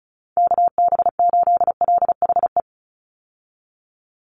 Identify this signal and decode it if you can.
Morse code, heard as a keyed tone, about 34 wpm, 700 Hz. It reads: KB8LHE
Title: USA Call Signs 35wpm